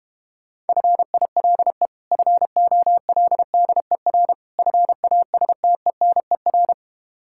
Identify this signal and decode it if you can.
Morse code, heard as a keyed tone, about 32 wpm, 700 Hz. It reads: FILE FOLDER FASTENER